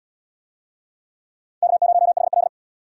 Morse code, heard as a keyed tone, about 72 wpm, 700 Hz. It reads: K0HL